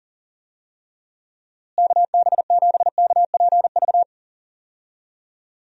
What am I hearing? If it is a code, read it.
Morse code, heard as a keyed tone, about 40 wpm, 700 Hz. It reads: KB7KPV
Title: USA Call Signs - 6 Characters - Repeated 3x - 40wpm